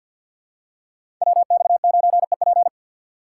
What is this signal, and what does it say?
Morse code, heard as a keyed tone, about 50 wpm, 700 Hz. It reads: WX9EP
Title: USA Call Signs - 4x Character Spacing - 50wpm